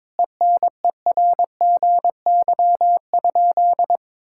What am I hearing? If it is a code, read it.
Morse code, heard as a keyed tone, about 22 wpm, 700 Hz. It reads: ENERGY?
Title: Sentences from Top 1000 Words - Review Part 6 - 22wpm